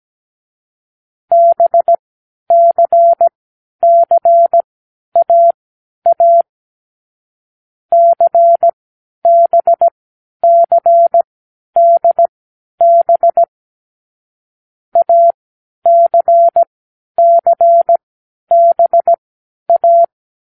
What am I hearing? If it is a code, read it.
Morse code, heard as a keyed tone, about 17 wpm, spaced as for 10 wpm, 680 Hz. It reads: BCCAA CBCDB ACCBA
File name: k7qo / Lesson6(A-D)